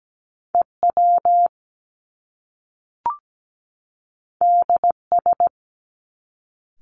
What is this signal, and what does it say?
Morse code, heard as a keyed tone, about 17 wpm, 700 Hz. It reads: EW DS
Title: Top Two Letter Combinations in Words 17wpm